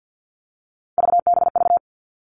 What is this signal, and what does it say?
Morse code, heard as a keyed tone, about 50 wpm, 700 Hz. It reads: VBV
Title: All Cut Numbers - 3-Digit 50wpm